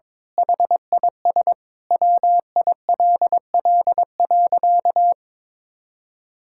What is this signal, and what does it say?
Morse code, heard as a keyed tone, about 22 wpm, 700 Hz. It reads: HIS WILL.
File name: Sentences with 9-Word Phrases - 22wpm